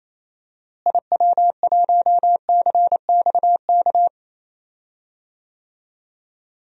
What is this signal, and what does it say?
Morse code, heard as a keyed tone, about 28 wpm, 700 Hz. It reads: IW1CXK
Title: DXCC-100 Call Signs - 6 Characters - Repeated 2x - 28wpm